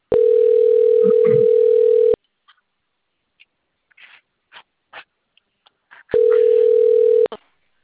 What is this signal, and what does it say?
Ringback tone (North America), 440 + 480 Hz.